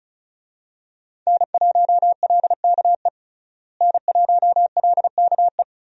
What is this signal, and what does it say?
Morse code, heard as a keyed tone, about 34 wpm, 700 Hz. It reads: N1LKE N1LKE
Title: USA Call Signs - 5 Characters - Repeated 2x - 35wpm